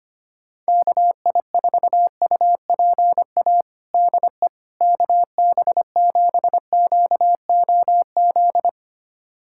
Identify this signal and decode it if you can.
Morse code, heard as a keyed tone, about 25 wpm, 700 Hz. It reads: KI4UPA DE KB7QOZ
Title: USA Callsign DE Callsign 25wpm